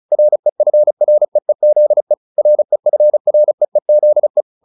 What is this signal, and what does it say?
Morse code, heard as a keyed tone, about 34 wpm, 600 Hz. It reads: REFREEZE REFREEZE